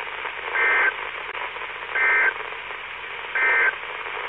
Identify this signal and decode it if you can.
Emergency Alert System (SAME) tones, heard as harsh buzzing bursts. It is heard from 0.4 to 3.8 s.
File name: Washtenaw County SKYWARN N8DUY and W8SRC Repeaters